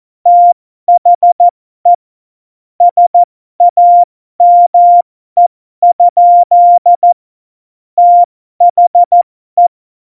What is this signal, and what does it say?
Morse code, heard as a keyed tone, about 14 wpm, spaced as for 12 wpm, 700 Hz. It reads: THE SAME? THE
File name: Morsekurs / Witze